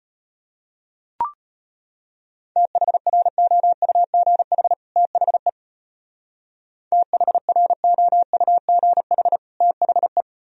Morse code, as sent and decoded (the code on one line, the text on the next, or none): - .... .-. --- ..- --. .... / - .... / - .... .-. --- ..- --. .... / - .... .
THROUGH THE THROUGH THE